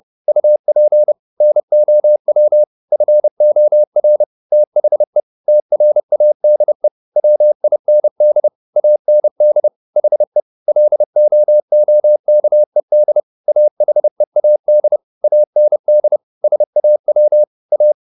Morse code, heard as a keyed tone, about 30 wpm, 600 Hz. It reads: UP NOW FOR THE TRADE WIND AND HE LOOKED AHEAD AND SAW A
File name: the old man and the sea / the old man and the sea 1 30wpm